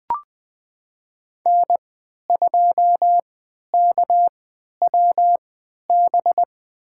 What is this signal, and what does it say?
Morse code, heard as a keyed tone, about 20 wpm, spaced as for 11 wpm, 700 Hz. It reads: N2KWB
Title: USA Call Signs - 5 Characters - 3x Character Spacing - 20wpm